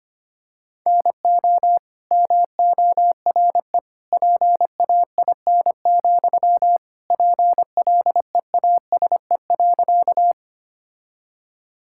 Morse code, heard as a keyed tone, about 25 wpm, 700 Hz. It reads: NO MORE PAIN, PLEASE.
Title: Sentences from Top 1600 Words 25wpm